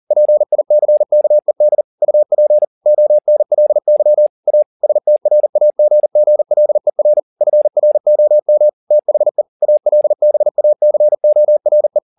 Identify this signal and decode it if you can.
Morse code, heard as a keyed tone, about 40 wpm, 600 Hz. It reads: PICKED UP ONLY A STRAGGLER FROM THE ALBACORE